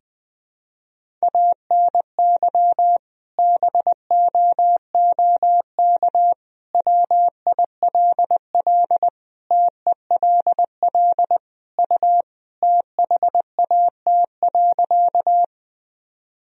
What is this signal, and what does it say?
Morse code, heard as a keyed tone, about 20 wpm, 700 Hz. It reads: ANY BOOK WILL TELL U THAT.
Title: Sentences from Top 300 Words and Abbreviations 20wpm